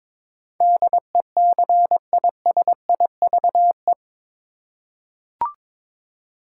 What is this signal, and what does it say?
Morse code, heard as a keyed tone, about 22 wpm, 700 Hz. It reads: DECISIVE
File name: practice / 22wpm / Eight Letter Words - 22wpm